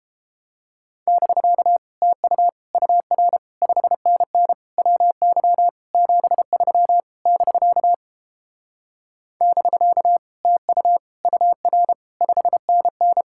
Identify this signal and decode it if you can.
Morse code, heard as a keyed tone, about 29 wpm, 700 Hz. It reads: <BK> TU UR 5NN WY 73 <BK> <BK> TU UR 5NN